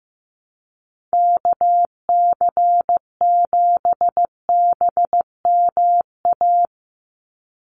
Morse code, heard as a keyed tone, about 15 wpm, 700 Hz. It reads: KC7BMA
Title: USA Call Signs 15wpm